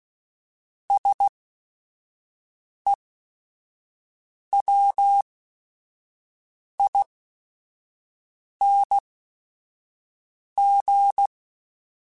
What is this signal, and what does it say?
Morse code, heard as a keyed tone, about 16 wpm, spaced as for 5 wpm, 800 Hz. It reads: SEWING